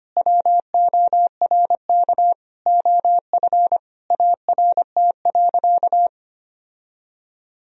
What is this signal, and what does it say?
Morse code, heard as a keyed tone, about 25 wpm, 700 Hz. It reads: WORK OF ART.
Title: Sentences from Top 1000 Words - Review Part 3 - 25wpm